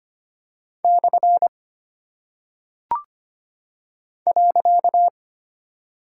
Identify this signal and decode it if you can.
Morse code, heard as a keyed tone, about 25 wpm, 700 Hz. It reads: / .
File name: practice / 25wpm / Single Punctuation-Prosign - 25wpm